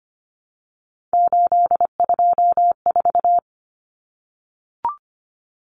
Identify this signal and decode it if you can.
Morse code, heard as a keyed tone, about 25 wpm, 700 Hz. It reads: 824
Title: Numbers - 3-Digit 25wpm